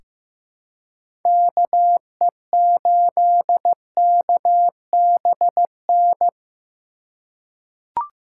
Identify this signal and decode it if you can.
Morse code, heard as a keyed tone, about 15 wpm, 700 Hz. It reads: KE8KBN
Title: USA Call Signs - 6x Character Spacing - 15wpm